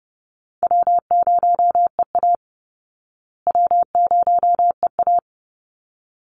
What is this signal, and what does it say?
Morse code, heard as a keyed tone, about 30 wpm, 700 Hz. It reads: W0EA W0EA
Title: Easy Call Signs Repeated 2x - 30wpm